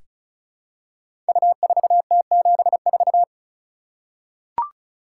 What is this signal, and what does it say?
Morse code, heard as a keyed tone, about 36 wpm, 700 Hz. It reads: U4T74
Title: Code Groups - 5 Letter:Numbers - 35wpm